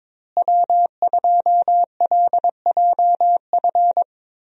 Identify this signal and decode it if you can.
Morse code, heard as a keyed tone, about 22 wpm, 700 Hz. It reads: W2LJF
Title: USA Call Signs - 5 Characters - Repeated 2x - 22wpm